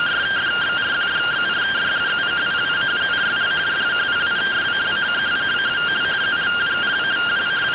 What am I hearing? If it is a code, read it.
Two-tone teleprinter signal, about 170 Hz shift.